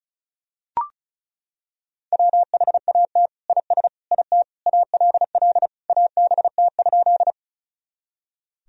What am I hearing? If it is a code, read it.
Morse code, heard as a keyed tone, about 34 wpm, 700 Hz. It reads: WHAT IS IT ALL ABT?
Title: Sentences from Top 100 Words and Abbreviations 35wpm